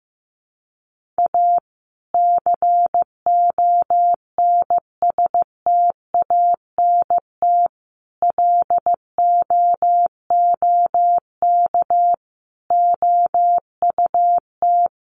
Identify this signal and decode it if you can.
Morse code, heard as a keyed tone, about 15 wpm, 700 Hz. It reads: A CONSTANT LOOK OUT
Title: Sentences from Top 1500 Words 15wpm